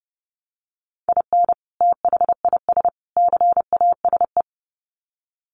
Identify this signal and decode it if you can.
Morse code, heard as a keyed tone, about 30 wpm, 700 Hz. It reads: IN THIS CASE
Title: Sets of 3 Words 30wpm